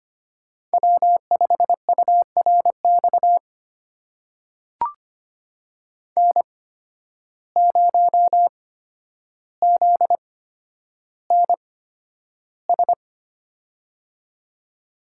Morse code, heard as a keyed tone, about 25 wpm, spaced as for 11 wpm, 700 Hz. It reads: W5URX N0ZNS
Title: USA Call Signs - 5 Characters - 8x Character Spacing - 25wpm